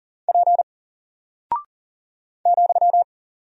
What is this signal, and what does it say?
Morse code, heard as a keyed tone, about 39 wpm, 700 Hz. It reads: P ,
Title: Single Character (Letter:Number:Punctuation:Prosign) 40wpm